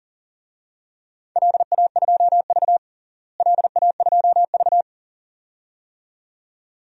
Morse code, heard as a keyed tone, about 40 wpm, 700 Hz. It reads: LA2V LA2V
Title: DXCC-100 Call Signs - 4 Characters - Repeated 2x - 40wpm